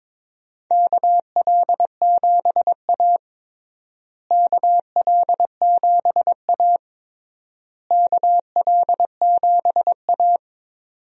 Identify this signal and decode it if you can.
Morse code, heard as a keyed tone, about 22 wpm, 700 Hz. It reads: KL7A KL7A KL7A
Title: DXCC-100 Call Signs - 4 Characters - Repeated 3x - 22wpm